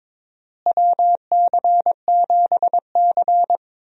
Morse code, heard as a keyed tone, about 21 wpm, 700 Hz. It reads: WC7C
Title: USA Call Signs - 4 Characters - Repeated 3x - 22wpm